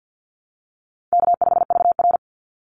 Morse code, heard as a keyed tone, about 50 wpm, 700 Hz. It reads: K5VR
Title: Easy Call Signs Repeated 3x - 50wpm